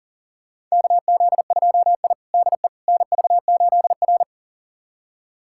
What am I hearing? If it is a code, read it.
Morse code, heard as a keyed tone, about 40 wpm, 700 Hz. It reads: KZ2I DE NV8R
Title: USA Callsign DE Callsign - 4 Characters - 40wpm